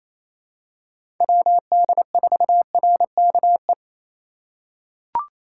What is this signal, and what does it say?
Morse code, heard as a keyed tone, about 28 wpm, 700 Hz. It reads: WD4RKE
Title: USA Call Signs - 6 Characters - 4x Character Spacing - 28wpm